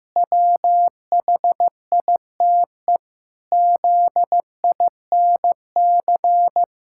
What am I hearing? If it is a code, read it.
Morse code, heard as a keyed tone, about 15 wpm, 700 Hz. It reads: WHITE ZINC
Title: Pangram Challenge v2 - 15wpm